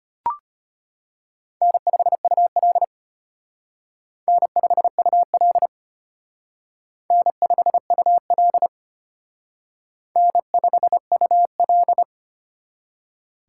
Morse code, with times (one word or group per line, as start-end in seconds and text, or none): 1.6-2.8 N5UL
4.3-5.7 N5UL
7.1-8.7 N5UL
10.2-12.0 N5UL